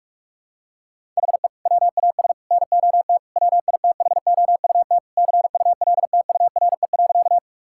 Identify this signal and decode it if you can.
Morse code, heard as a keyed tone, about 46 wpm, 700 Hz. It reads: HE WAS NOT WITHOUT CULTURE.
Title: Sentences from Top 1000 Words - Review Part 6 - 45wpm